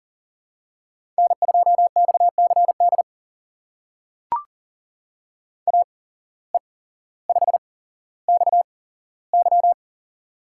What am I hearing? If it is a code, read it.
Morse code, heard as a keyed tone, about 40 wpm, spaced as for 17 wpm, 700 Hz. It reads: N2XCD AE5XY
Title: USA Call Signs - 5 Characters - 8x Character Spacing - 40wpm